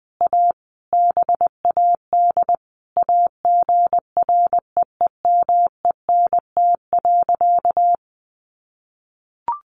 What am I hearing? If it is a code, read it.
Morse code, heard as a keyed tone, about 20 wpm, 700 Hz. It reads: A BAD AGREEMENT.
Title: Sentences from Top 1500 Words 20wpm